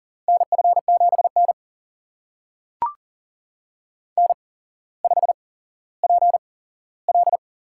Morse code, heard as a keyed tone, about 40 wpm, spaced as for 17 wpm, 700 Hz. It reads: NF7N N5PL
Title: USA Call Signs - 4 Characters - 8x Character Spacing - 40wpm